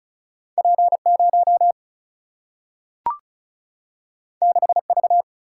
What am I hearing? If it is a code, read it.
Morse code, heard as a keyed tone, about 36 wpm, 700 Hz. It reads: P0 6V
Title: Single-Letter Single-Number 35wpm